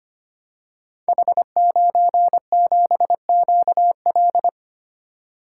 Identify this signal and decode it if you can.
Morse code, heard as a keyed tone, about 25 wpm, 700 Hz. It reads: H97QL
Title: Code Groups - 5 Letter:Numbers - 25wpm